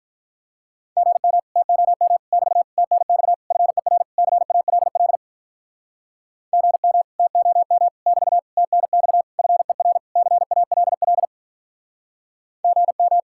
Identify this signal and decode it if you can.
Morse code, heard as a keyed tone, about 46 wpm, 700 Hz. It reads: GM TOM <BT> TNX FER CALL GM TOM <BT> TNX FER CALL GM